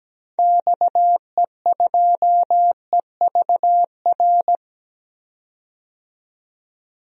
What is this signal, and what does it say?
Morse code, heard as a keyed tone, about 17 wpm, 700 Hz. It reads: XE2EVR
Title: DXCC-100 Call Signs - 6 Characters - 17wpm